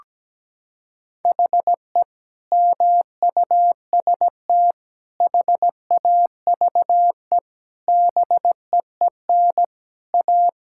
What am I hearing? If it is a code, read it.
Morse code, heard as a keyed tone, about 17 wpm, 700 Hz. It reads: HE MUST HAVE BEEN A
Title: Sentences from Top 1600 Words 17wpm